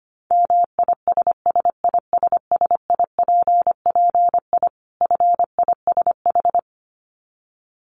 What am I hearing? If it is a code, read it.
Morse code, heard as a keyed tone, about 25 wpm, 700 Hz. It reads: MISSISSIPPI FISH
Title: Dit Challenge 25wpm